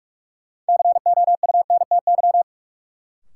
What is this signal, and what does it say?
Morse code, heard as a keyed tone, about 45 wpm, 700 Hz. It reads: COUNTY
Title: Sentences from Top 1200 Words - New Word Drill - 45wpm